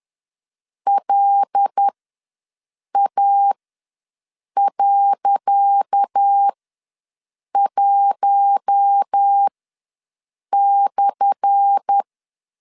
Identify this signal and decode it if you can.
Morse code, heard as a keyed tone, about 10 wpm, spaced as for 6 wpm, 800 Hz. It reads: LA.1/
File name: LEZIONI / 10 SIGNS / 10C DOT ws 10wpm 4213